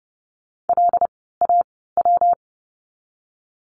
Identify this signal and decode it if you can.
Morse code, heard as a keyed tone, about 30 wpm, spaced as for 17 wpm, 700 Hz. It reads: LAW
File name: course / 30:10wpm / 115 Lesson 19f - Letter w - Words 30wpm